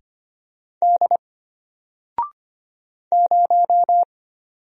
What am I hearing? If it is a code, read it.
Morse code, heard as a keyed tone, about 25 wpm, 700 Hz. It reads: D 0
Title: Single Letter-Number Alt 25wpm